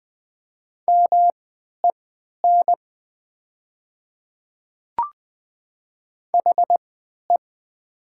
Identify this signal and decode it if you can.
Morse code, heard as a keyed tone, about 20 wpm, spaced as for 11 wpm, 700 Hz. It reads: MEN HE
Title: 3x Character Spacing - Top 200 Words - 20wpm